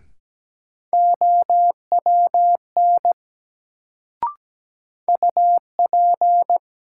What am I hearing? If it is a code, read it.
Morse code, heard as a keyed tone, about 17 wpm, 700 Hz. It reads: OWN UP
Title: Sets of 2 Words - New Word Drill - 17wpm